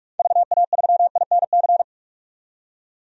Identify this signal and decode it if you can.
Morse code, heard as a keyed tone, about 47 wpm, 700 Hz. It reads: VA3INC
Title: DXCC-100 Call Signs - 6 Characters - 45wpm